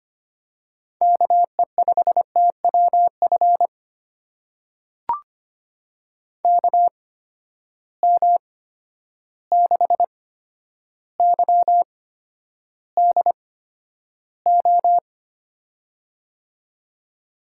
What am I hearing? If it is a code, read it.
Morse code, heard as a keyed tone, about 25 wpm, spaced as for 11 wpm, 700 Hz. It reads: KE5TWF KM6YDO
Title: USA Call Signs - 6 Characters - 8x Character Spacing - 25wpm